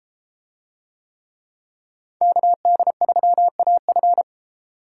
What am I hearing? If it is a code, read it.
Morse code, heard as a keyed tone, about 32 wpm, 700 Hz. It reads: KD3AF